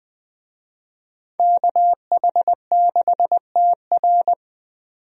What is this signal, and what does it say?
Morse code, heard as a keyed tone, about 21 wpm, 700 Hz. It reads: KH6TR
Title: DXCC-100 Call Signs - 5 Characters - 20wpm